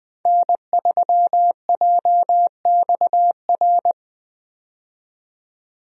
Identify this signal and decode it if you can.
Morse code, heard as a keyed tone, about 20 wpm, 700 Hz. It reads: N3JXR
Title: USA Call Signs - 5 Characters - 20wpm